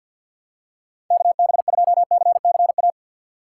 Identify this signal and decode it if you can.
Morse code, heard as a keyed tone, about 53 wpm, 700 Hz. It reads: KB2CCA
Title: USA Call Signs - 6 Characters - 3x Character Spacing - 50wpm